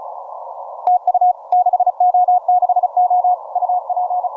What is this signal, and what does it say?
Morse code, heard as a keyed tone, about 36 wpm, 720 Hz. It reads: TU 6O6O UP
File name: simple-file-list / 6O6O 15MCW